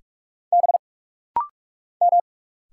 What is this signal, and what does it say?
Morse code, heard as a keyed tone, about 45 wpm, 700 Hz. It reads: B M
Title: Single Letters - Rapid-Fire 45wpm